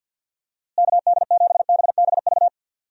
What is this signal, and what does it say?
Morse code, heard as a keyed tone, about 51 wpm, 700 Hz. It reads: KD7BBV
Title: USA Call Signs - 6 Characters - Repeated 2x - 50wpm